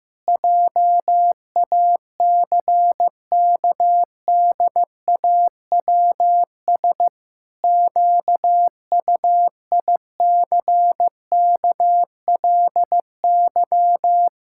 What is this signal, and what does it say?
Morse code, heard as a keyed tone, about 15 wpm, 700 Hz. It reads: JACKDAWS QUICKLY
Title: Pangram Challenge v2 - 15wpm